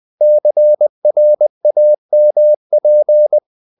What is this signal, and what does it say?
Morse code, heard as a keyed tone, about 20 wpm, 600 Hz. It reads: CRAMP
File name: the old man and the sea / the old man and the sea part 1 20wpm